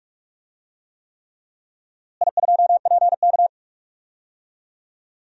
Morse code, heard as a keyed tone, about 43 wpm, 700 Hz. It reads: I2PK
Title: DXCC-100 Call Signs - 4 Characters - Repeated 3x - 45wpm